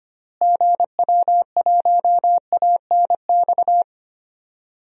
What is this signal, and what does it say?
Morse code, heard as a keyed tone, about 25 wpm, 700 Hz. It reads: GW1ANX
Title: DXCC-100 Call Signs - 6 Characters - Repeated 3x - 25wpm